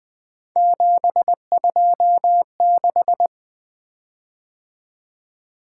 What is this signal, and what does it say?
Morse code, heard as a keyed tone, about 20 wpm, 700 Hz. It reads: 726
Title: Three Letter-Number Combinations - 20wpm